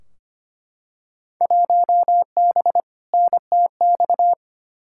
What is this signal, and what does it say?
Morse code, heard as a keyed tone, about 25 wpm, 700 Hz. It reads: 1B NTX
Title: Field Day Exchanges - 25wpm